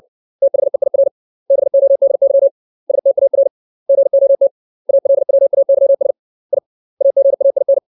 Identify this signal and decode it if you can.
Morse code, heard as a keyed tone, about 65 wpm, 550 Hz. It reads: THEIR BODY STAR GOT ALWAYS I AGAIN
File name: 632 words / 632 Words 60WPM